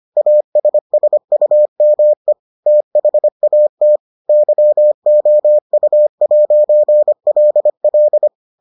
Morse code, heard as a keyed tone, about 25 wpm, 600 Hz. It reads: ASSUME THAT YOU'LL